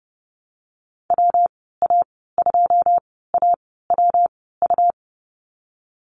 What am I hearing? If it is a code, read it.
Morse code, heard as a keyed tone, about 30 wpm, spaced as for 17 wpm, 700 Hz. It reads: WA2AWU
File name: Lesson 37g - Letter x - Call Signs 30wpm